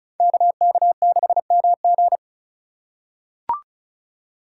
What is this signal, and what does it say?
Morse code, heard as a keyed tone, about 36 wpm, 700 Hz. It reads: KK6MG
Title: USA Call Signs - 5 Characters - 4x Character Spacing - 35wpm